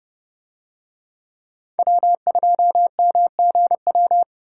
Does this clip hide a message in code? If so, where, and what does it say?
Yes: Morse code from 1.8 to 4.2 s, heard as a keyed tone, about 30 wpm, 700 Hz. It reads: W2MGW